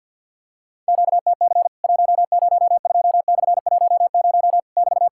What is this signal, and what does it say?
Morse code, heard as a keyed tone, about 50 wpm, 700 Hz. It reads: QTC 102/10 <BT>
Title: Worked All Europe DX Contest - QTC Exchange - 50wpm